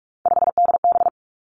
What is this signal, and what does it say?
Morse code, heard as a keyed tone, about 46 wpm, 700 Hz. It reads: 5DB
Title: Three Challenging Letter-Number Combinations 45wpm